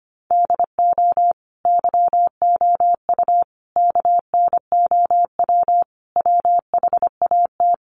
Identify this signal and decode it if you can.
Morse code, heard as a keyed tone, about 25 wpm, 700 Hz. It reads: DO YOU KNOW WHAT
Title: Sentences from Top 1500 Words 25wpm